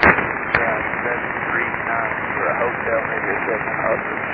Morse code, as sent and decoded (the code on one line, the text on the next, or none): none